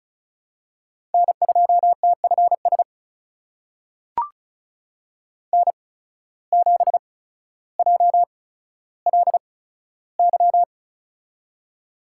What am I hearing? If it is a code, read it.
Morse code, heard as a keyed tone, about 36 wpm, spaced as for 15 wpm, 700 Hz. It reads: N2TFS N7JLY
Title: USA Call Signs - 5 Characters - 8x Character Spacing - 35wpm